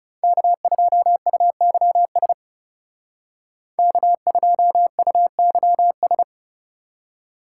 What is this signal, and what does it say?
Morse code, heard as a keyed tone, about 39 wpm, 700 Hz. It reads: K2UYS K2UYS